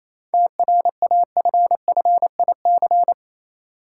Morse code, heard as a keyed tone, about 28 wpm, 700 Hz. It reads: TRAFFIC
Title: Sentences from Top 1300 Words - New Word Drill Alt - 28wpm